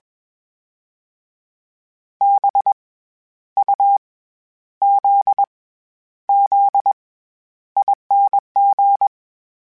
Morse code, heard as a keyed tone, about 21 wpm, 800 Hz. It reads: B U Z Z ING
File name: Special INGs-mp3s / Special INGs-mp3s / SlowToFaster ing @20